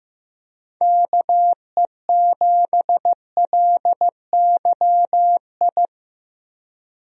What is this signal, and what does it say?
Morse code, heard as a keyed tone, about 16 wpm, 700 Hz. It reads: KE7LYI
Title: USA Call Signs - 6 Characters - Repeated 2x - 15wpm